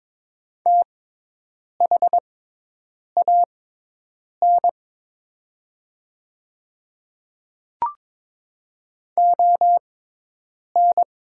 Morse code, heard as a keyed tone, about 22 wpm, spaced as for 8 wpm, 700 Hz. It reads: THAN ON
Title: 6x Character Spacing - Top 200 Words - 22wpm